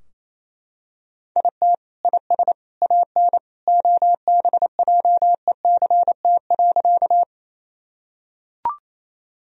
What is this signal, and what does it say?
Morse code, heard as a keyed tone, about 27 wpm, 700 Hz. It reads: IT IS AN OBJECT.
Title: Sentences from Top 1000 Words - Review Part 3 - 28wpm